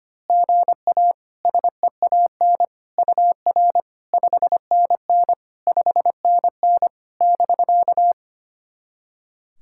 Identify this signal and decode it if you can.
Morse code, heard as a keyed tone, about 25 wpm, 700 Hz. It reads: GA SEAN UR 5NN 5NN <BK>
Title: SOTA QSO Practice 25wpm